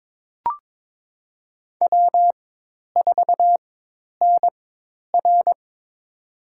Morse code, heard as a keyed tone, about 22 wpm, spaced as for 10 wpm, 700 Hz. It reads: W4NR
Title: USA Call Signs - 4x Character Spacing - 22wpm